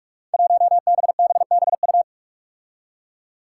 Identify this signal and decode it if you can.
Morse code, heard as a keyed tone, about 46 wpm, 700 Hz. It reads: 1BBBU
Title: Five Challenging Letter-Number Combinations 45wpm